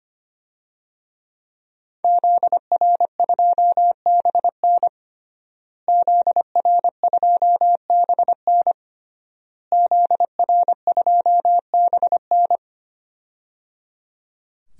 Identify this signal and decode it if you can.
Morse code, heard as a keyed tone, about 25 wpm, 700 Hz. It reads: ZR2BN ZR2BN ZR2BN
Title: DXCC-100 Call Signs - 5 Characters - Repeated 3x - 25wpm